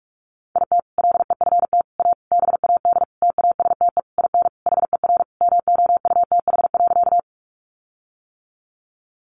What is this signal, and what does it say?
Morse code, heard as a keyed tone, about 44 wpm, 700 Hz. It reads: IT LEFT A BAD TASTE IN HER MOUTH.
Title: Sentences from Top 1500 Words 45wpm